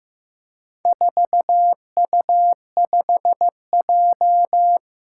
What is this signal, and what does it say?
Morse code, heard as a keyed tone, about 15 wpm, 700 Hz. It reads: 4U5J
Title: Four Challenging Letter-Number Combinations 15wpm